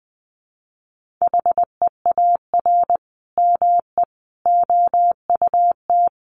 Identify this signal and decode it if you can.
Morse code, heard as a keyed tone, about 20 wpm, 700 Hz. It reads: HEAR ME OUT
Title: Sentences from Top 1400 Words 20wpm